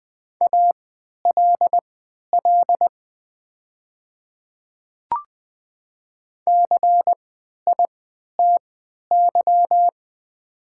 Morse code, heard as a keyed tone, about 20 wpm, spaced as for 11 wpm, 700 Hz. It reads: ALL CITY
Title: 3x Character Spacing - Top 300 Words - 20wpm